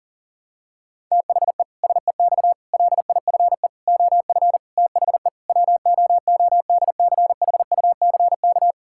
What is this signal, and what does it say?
Morse code, heard as a keyed tone, about 40 wpm, 700 Hz. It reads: THE SEX LIFE OF THE WOODCHUCK
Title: Pangram Challenge v2 - 40wpm